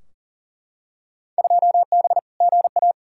Morse code, heard as a keyed tone, about 40 wpm, 700 Hz. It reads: 2B GA
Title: Field Day Exchanges Encore - 40wpm